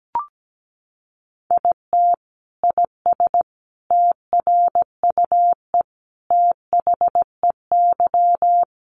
Morse code, heard as a keyed tone, about 17 wpm, 700 Hz. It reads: IT IS TRUE THEY
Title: Sentences from Top 1400 Words 17wpm